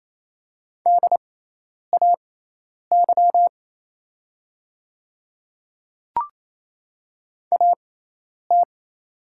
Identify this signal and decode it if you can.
Morse code, heard as a keyed tone, about 27 wpm, spaced as for 10 wpm, 690 Hz. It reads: DAY AT
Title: 6x Character Spacing - Top 100 Words - 28wpm